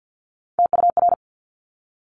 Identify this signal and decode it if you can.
Morse code, heard as a keyed tone, about 50 wpm, 700 Hz. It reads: TUR
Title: Top Three Letter Combinations in Words 50wpm